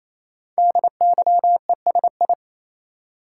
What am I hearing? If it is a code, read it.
Morse code, heard as a keyed tone, about 28 wpm, 700 Hz. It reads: DYESI
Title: Code Groups - 5 Letters - 28wpm